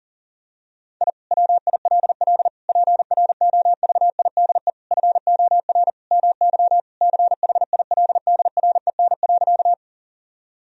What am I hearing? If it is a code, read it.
Morse code, heard as a keyed tone, about 40 wpm, 700 Hz. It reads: I WILL PROVIDE FOR MY CHILDREN.
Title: Sentences from Top 1000 Words - Review Part 3 - 40wpm